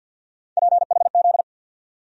Morse code, heard as a keyed tone, about 47 wpm, 690 Hz. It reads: PHZ
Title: USA Call Sign Postfixes - 3 Letters 50wpm